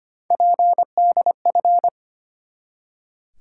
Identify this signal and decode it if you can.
Morse code, heard as a keyed tone, about 25 wpm, 700 Hz. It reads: PDF
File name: practice / 25wpm / Easy Words with Hints - 25wpm